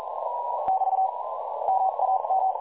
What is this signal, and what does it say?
Morse code, heard as a keyed tone, about 38 wpm, spaced as for 34 wpm, 760 Hz, with noise. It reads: <BT> MAM
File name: dlazaawansowanych / nagraniazpasma / websdr recording 3565.0kHz